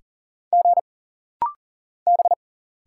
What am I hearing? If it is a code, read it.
Morse code, heard as a keyed tone, about 39 wpm, 700 Hz. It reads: G B
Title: Single Letter-Number Alt - Rapid-Fire 40wpm